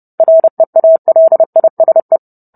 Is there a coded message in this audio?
Yes: Morse code from 0.2 to 2.2 s, heard as a keyed tone, about 30 wpm, 650 Hz. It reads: REALISE